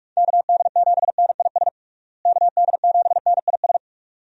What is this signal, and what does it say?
Morse code, heard as a keyed tone, about 41 wpm, 700 Hz. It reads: KD7NIS KD7NIS
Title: USA Call Signs - 6 Characters - Repeated 2x - 45wpm